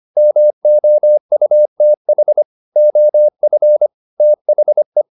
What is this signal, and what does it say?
Morse code, heard as a keyed tone, about 25 wpm, 600 Hz. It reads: MOUTH OF THE
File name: the old man and the sea part 1 25wpm